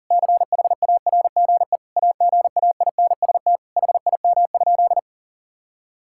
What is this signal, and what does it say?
Morse code, heard as a keyed tone, about 40 wpm, 700 Hz. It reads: CHARGE AGAINST HIM?
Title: Sentences from Top 1000 Words - Review Part 6 - 40wpm